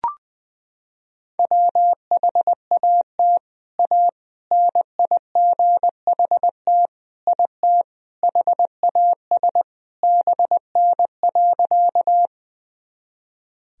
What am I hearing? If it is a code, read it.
Morse code, heard as a keyed tone, about 21 wpm, 700 Hz. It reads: WHAT A NIGHT IT HAS BN.
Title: Sentences from Top 300 Words and Abbreviations 20wpm